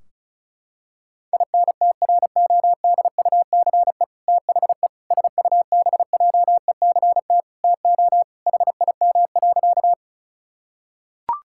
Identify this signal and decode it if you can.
Morse code, heard as a keyed tone, about 36 wpm, 700 Hz. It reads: INTRODUCE THE SUBJECT TO HIM.
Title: Sentences from Top 1600 Words 35wpm